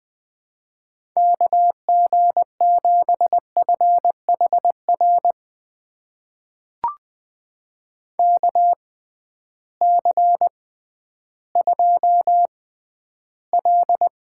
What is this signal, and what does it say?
Morse code, heard as a keyed tone, about 20 wpm, spaced as for 12 wpm, 700 Hz. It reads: KG7FHR KC2L
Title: USA Call Signs - 6 Characters - 6x Character Spacing - 20wpm